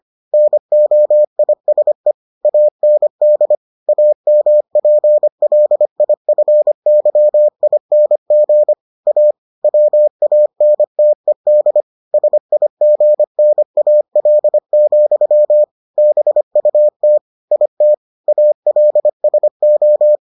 Morse code, heard as a keyed tone, about 25 wpm, 600 Hz. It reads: NOISE AND AMPLIFYING A WANTED SIGNAL, BUT IT ALSO